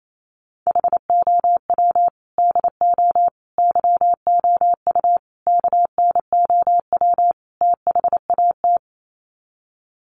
Morse code, heard as a keyed tone, about 28 wpm, 690 Hz. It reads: HOW DO YOU KNOW THAT
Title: Sets of 5 Words Encore 28wpm